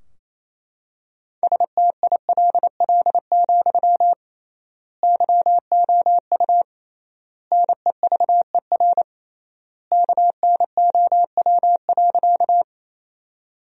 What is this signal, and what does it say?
Morse code, heard as a keyed tone, about 27 wpm, 700 Hz. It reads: STILL, YOU NEVER KNOW.
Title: 3x Word-Spacing - Sentences from Top 300 Words - 28wpm